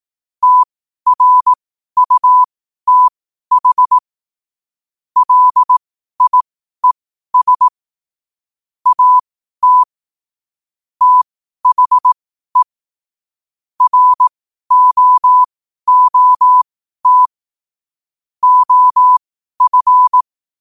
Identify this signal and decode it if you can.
Morse code, heard as a keyed tone, about 18 wpm, spaced as for 12 wpm, 1000 Hz. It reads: TRUTH LIES AT THE ROOT OF
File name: qotdfiles / 12wpm